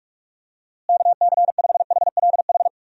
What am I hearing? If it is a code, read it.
Morse code, heard as a keyed tone, about 44 wpm, 700 Hz. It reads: KC5HLH
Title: USA Call Signs - 6 Characters - Repeated 2x - 45wpm